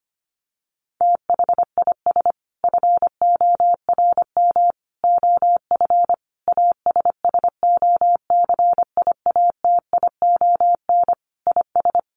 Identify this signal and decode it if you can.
Morse code, heard as a keyed tone, about 25 wpm, 700 Hz. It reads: THIS FORM OF ASSOCIATION IS
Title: Sentences from Top 1100 Words 25wpm